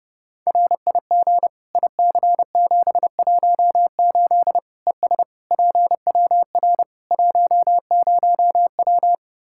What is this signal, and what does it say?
Morse code, heard as a keyed tone, about 32 wpm, 700 Hz. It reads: RIG IC718 ES PWR 10W
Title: Practice QSO - Exchange 2 - Info - 30wpm